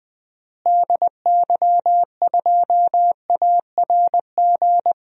Morse code, heard as a keyed tone, about 20 wpm, 700 Hz. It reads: DY2ARG
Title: DXCC-100 Call Signs - 6 Characters - Repeated 3x - 20wpm